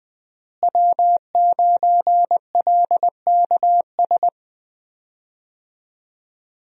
Morse code, heard as a keyed tone, about 20 wpm, 700 Hz. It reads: W9LKS
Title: USA Call Signs - 5 Characters - Repeated 2x - 20wpm